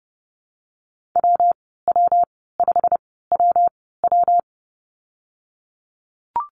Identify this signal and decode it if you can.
Morse code, heard as a keyed tone, about 30 wpm, spaced as for 17 wpm, 700 Hz. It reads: WW5WW
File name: Lesson 19g - Letter w - Call Signs 30wpm